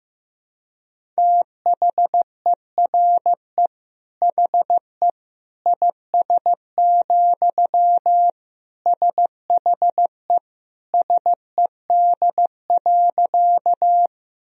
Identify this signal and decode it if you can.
Morse code, heard as a keyed tone, about 15 wpm, 700 Hz. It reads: THERE HE IS, SHE SED.
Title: Sentences from Top 100 Words and Abbreviations 15wpm